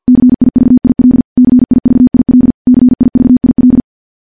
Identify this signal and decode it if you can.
Morse code, heard as a keyed tone, about 50 wpm, 250 Hz. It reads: CIVIL CIVIL CIVIL
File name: morse / morse-training-50WPM